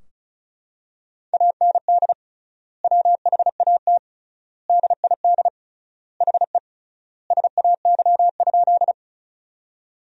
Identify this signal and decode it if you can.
Morse code, heard as a keyed tone, about 36 wpm, 700 Hz. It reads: AND WHAT DID HE SAY?
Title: 3x Word-Spacing - Sentences from Top 200 Words - 35wpm